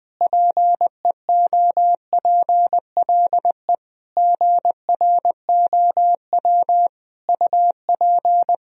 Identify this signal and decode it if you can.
Morse code, heard as a keyed tone, about 21 wpm, 700 Hz. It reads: PEOPLE GROW UP
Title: Sentences from Top 1000 Words - Review Part 1 - 20wpm